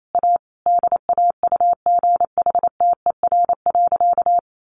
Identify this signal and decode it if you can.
Morse code, heard as a keyed tone, about 28 wpm, 700 Hz. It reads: A DAUGHTER.
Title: Sentences from Top 1500 Words 28wpm